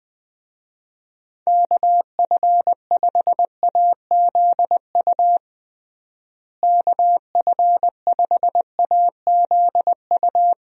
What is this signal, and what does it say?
Morse code, heard as a keyed tone, about 20 wpm, 700 Hz. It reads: KF5AZU KF5AZU
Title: USA Call Signs - 6 Characters - Repeated 2x - 20wpm